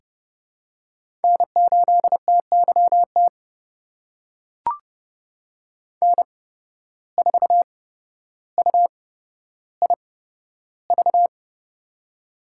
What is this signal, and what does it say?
Morse code, heard as a keyed tone, about 31 wpm, spaced as for 13 wpm, 700 Hz. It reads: N8TYT N4UIV
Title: USA Call Signs - 5 Characters - 8x Character Spacing - 30wpm